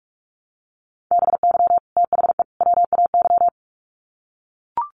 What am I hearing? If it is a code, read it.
Morse code, heard as a keyed tone, about 44 wpm, 700 Hz. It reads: BY THE WAY
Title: Sets of 3 Words 45wpm